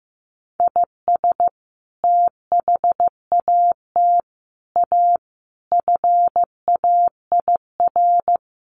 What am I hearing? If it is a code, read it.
Morse code, heard as a keyed tone, about 15 wpm, 700 Hz. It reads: IS THAT A FAIR
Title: Sentences from Top 1300 Words 15wpm